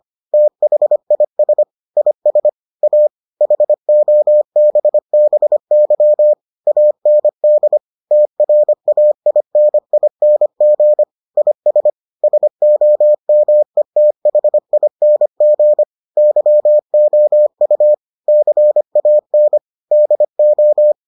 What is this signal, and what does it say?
Morse code, heard as a keyed tone, about 25 wpm, 600 Hz. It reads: THIS IS A HOBBY AND TRAINING IS SOMETHING YOU CAN DO